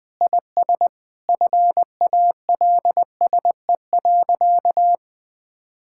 Morse code, heard as a keyed tone, about 20 wpm, 700 Hz. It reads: IS FALSE.
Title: Sentences from Top 1600 Words 20wpm